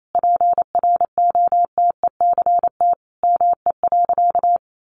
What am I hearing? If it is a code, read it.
Morse code, heard as a keyed tone, about 28 wpm, 700 Hz. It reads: PROTECT ME.
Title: Sentences from Top 1100 Words 28wpm